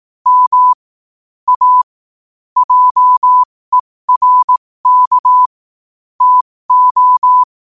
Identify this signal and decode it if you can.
Morse code, heard as a keyed tone, about 18 wpm, spaced as for 15 wpm, 1000 Hz. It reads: M A JERK TO